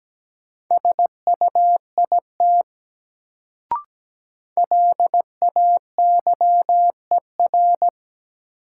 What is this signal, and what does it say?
Morse code, heard as a keyed tone, about 17 wpm, 700 Hz. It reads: SUIT LAYER
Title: Sentences from Top 1600 Words - New Word Drill - 17wpm